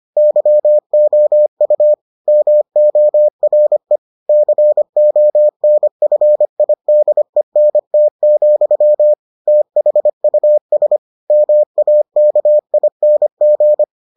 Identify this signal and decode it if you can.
Morse code, heard as a keyed tone, about 25 wpm, 600 Hz. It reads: YOU MORE CONFIDENT, THUS MAKING